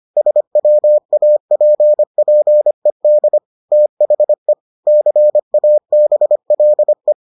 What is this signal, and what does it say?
Morse code, heard as a keyed tone, about 25 wpm, 600 Hz. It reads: SWAPPED THE CABLE